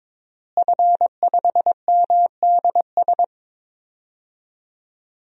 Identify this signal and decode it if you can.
Morse code, heard as a keyed tone, about 22 wpm, 700 Hz. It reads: F5MDS
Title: DXCC-100 Call Signs - 5 Characters - Repeated 3x - 22wpm